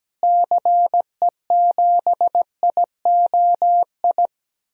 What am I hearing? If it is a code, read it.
Morse code, heard as a keyed tone, about 17 wpm, 700 Hz. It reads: CE7IOI
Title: DXCC-100 Call Signs - 6 Characters - Repeated 3x - 17wpm